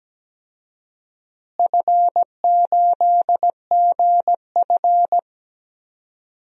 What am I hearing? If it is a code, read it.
Morse code, heard as a keyed tone, about 17 wpm, 700 Hz. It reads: F8GF
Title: DXCC-100 Call Signs - 4 Characters - Repeated 3x - 17wpm